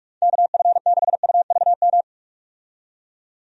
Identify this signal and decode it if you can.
Morse code, heard as a keyed tone, about 43 wpm, 700 Hz. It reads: KF6UVM
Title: USA Call Signs - 6 Characters - 45wpm